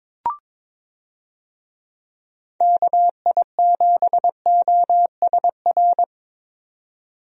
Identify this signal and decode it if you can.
Morse code, heard as a keyed tone, about 22 wpm, 700 Hz. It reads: KI7OSR
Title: USA Call Signs - 6 Characters - 22wpm